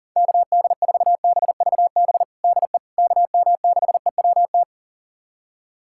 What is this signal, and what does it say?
Morse code, heard as a keyed tone, about 37 wpm, 700 Hz. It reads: KD4BVB DE KM6EWT
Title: USA Callsign DE Callsign - 6 Characters - 40wpm